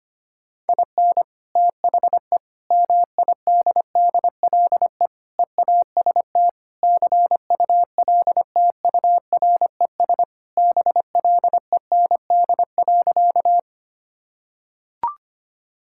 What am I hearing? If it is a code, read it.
Morse code, heard as a keyed tone, about 25 wpm, 700 Hz. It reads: IN THE MIDDLE EAST CULTURES BLEND.
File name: practice / 25wpm / Sentences with 4-Word Phrases - 25wpm